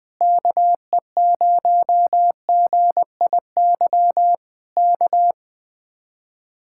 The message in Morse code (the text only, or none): KE0GIY K